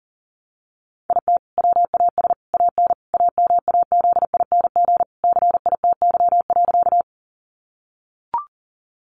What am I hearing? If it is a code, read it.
Morse code, heard as a keyed tone, about 37 wpm, 700 Hz. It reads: IT WAS AN AMAZING CITY.
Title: Sentences from Top 1500 Words 40wpm